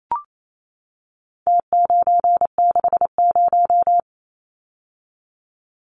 Morse code, heard as a keyed tone, about 28 wpm, 700 Hz. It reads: T960
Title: Numbers - Contest Style - 4-Digit 28wpm